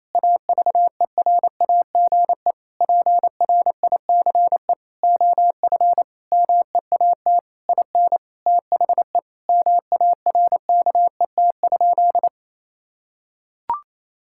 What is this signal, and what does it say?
Morse code, heard as a keyed tone, about 28 wpm, 700 Hz. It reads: AVERAGE PRICE OF MEAT IN THE MARKET?
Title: Sentences from Top 1000 Words - Review Part 6 - 28wpm